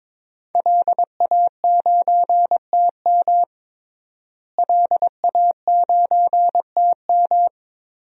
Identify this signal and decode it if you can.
Morse code, heard as a keyed tone, about 22 wpm, 700 Hz. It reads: LA9TM LA9TM
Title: DXCC-100 Call Signs - 5 Characters - Repeated 2x - 22wpm